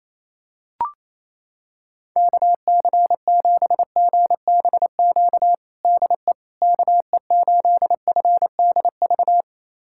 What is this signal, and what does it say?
Morse code, heard as a keyed tone, about 28 wpm, 700 Hz. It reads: KC7GBQ DE KE8FDV
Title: USA Callsign DE Callsign - 6 Characters - 28wpm